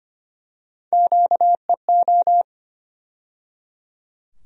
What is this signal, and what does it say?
Morse code, heard as a keyed tone, about 25 wpm, 700 Hz. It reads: QEO
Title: USA Call Sign Postfixes - 3 Letters 25wpm